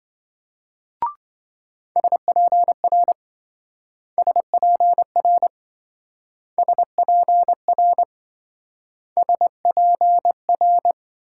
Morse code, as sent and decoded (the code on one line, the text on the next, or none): ... .--. .-. / ... .--. .-. / ... .--. .-. / ... .--. .-.
SPR SPR SPR SPR